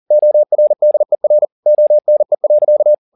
Morse code, heard as a keyed tone, about 40 wpm, 600 Hz. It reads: ORDER ONE.